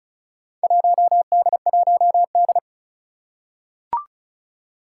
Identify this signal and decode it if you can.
Morse code, heard as a keyed tone, about 36 wpm, 700 Hz. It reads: 1D1D
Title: Four Challenging Letter-Number Combinations 35wpm